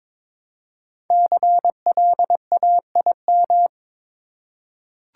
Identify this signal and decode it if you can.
Morse code, heard as a keyed tone, about 22 wpm, 700 Hz. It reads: CLAIM
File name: 5 Letter Words - First 3 Letters Given - 22wpm